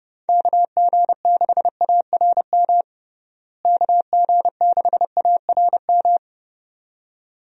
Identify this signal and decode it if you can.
Morse code, heard as a keyed tone, about 30 wpm, 700 Hz. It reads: KG6ARM KG6ARM
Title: USA Call Signs - 6 Characters - Repeated 2x - 30wpm